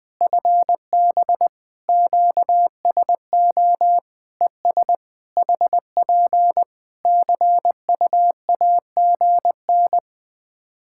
Morse code, heard as a keyed tone, about 20 wpm, 700 Hz. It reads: FB QSO ES HP CUAGN
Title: Practice QSO - Exchange 4 - Ending - 20wpm